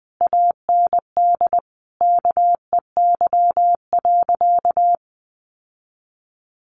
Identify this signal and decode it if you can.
Morse code, heard as a keyed tone, about 20 wpm, 700 Hz. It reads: AND KEY.
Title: Sentences from Top 1300 Words 20wpm